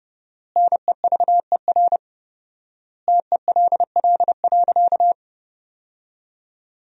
Morse code, heard as a keyed tone, about 30 wpm, 700 Hz. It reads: NEVER TELL.
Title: 4x Word-Spacing - Sentences from Top 300 Words - 30wpm